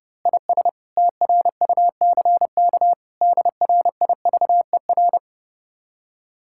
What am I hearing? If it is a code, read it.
Morse code, heard as a keyed tone, about 28 wpm, 700 Hz. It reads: IS TRUCK DRIVER
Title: Speed-Racing Common QSO Phrases Encore - 30wpm